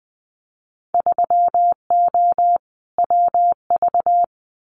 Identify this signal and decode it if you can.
Morse code, heard as a keyed tone, about 20 wpm, 700 Hz. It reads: 3O WV
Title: Winter Field Day Exchanges 20wpm